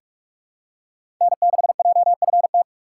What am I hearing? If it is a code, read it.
Morse code, heard as a keyed tone, about 46 wpm, 700 Hz. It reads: N6JFT
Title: USA Call Signs - 5 Characters - 6x Character Spacing - 45wpm